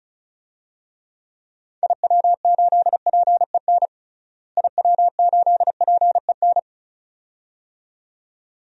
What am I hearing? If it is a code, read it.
Morse code, heard as a keyed tone, about 34 wpm, 700 Hz. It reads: IW8PEN IW8PEN
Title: DXCC-100 Call Signs - 6 Characters - Repeated 2x - 35wpm